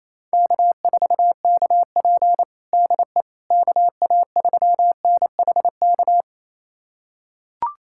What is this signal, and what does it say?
Morse code, heard as a keyed tone, about 28 wpm, 700 Hz. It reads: K4KP DE KA3NHK
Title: USA Callsign DE Callsign 28wpm